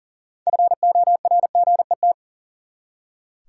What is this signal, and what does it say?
Morse code, heard as a keyed tone, about 40 wpm, 700 Hz. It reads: FORGET
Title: Six Letter Words - 40wpm